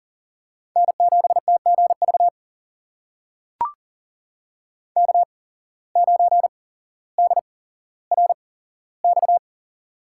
Morse code, heard as a keyed tone, about 40 wpm, spaced as for 17 wpm, 700 Hz. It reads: N7TGV K9DRX
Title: USA Call Signs - 5 Characters - 8x Character Spacing - 40wpm